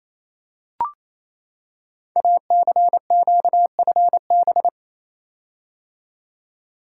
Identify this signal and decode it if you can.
Morse code, heard as a keyed tone, about 28 wpm, 700 Hz. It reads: ACQFB
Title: Code Groups - 5 Letters - 28wpm